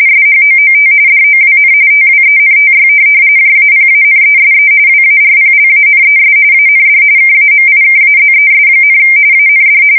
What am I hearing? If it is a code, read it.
RTTY, about 45.45 baud. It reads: SHOOT DOWN TWENTY JAPANESE WARPLANES AND KAMIKAZES, AND
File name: fsk170 / USS-New-Jersey